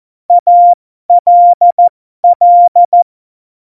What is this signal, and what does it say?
Morse code, heard as a keyed tone, about 14 wpm, spaced as for 12 wpm, 700 Hz. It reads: ALL
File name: Morsekurs / Witze